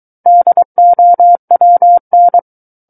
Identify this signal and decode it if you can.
Morse code, heard as a keyed tone, about 23 wpm, 700 Hz. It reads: DOWN
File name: Book One / 14-Chapter 14 - 23 WPM